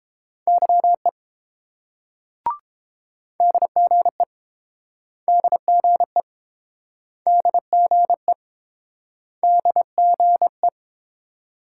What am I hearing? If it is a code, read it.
Morse code, heard as a keyed tone, about 30 wpm, 700 Hz. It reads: YE DGE DGE DGE DGE